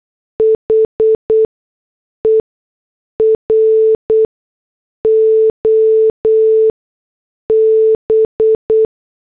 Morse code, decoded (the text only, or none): HER OB